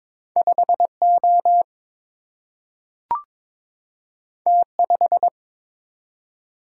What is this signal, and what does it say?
Morse code, heard as a keyed tone, about 22 wpm, 700 Hz. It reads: 5 O T 5
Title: Single-Letter Single-Number 22wpm